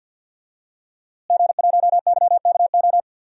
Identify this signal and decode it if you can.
Morse code, heard as a keyed tone, about 50 wpm, 700 Hz. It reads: G1YKO
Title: DXCC-100 Call Signs - 5 Characters - Repeated 3x - 50wpm